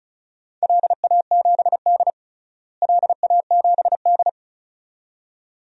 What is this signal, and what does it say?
Morse code, heard as a keyed tone, about 36 wpm, 690 Hz. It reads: LA7D LA7D
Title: DXCC-100 Call Signs - 4 Characters - Repeated 2x - 35wpm